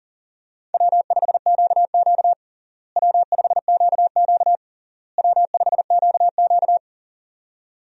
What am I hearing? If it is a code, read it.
Morse code, heard as a keyed tone, about 37 wpm, 700 Hz. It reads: W5QQ W5QQ W5QQ
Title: USA Call Signs - 4 Characters - Repeated 3x - 40wpm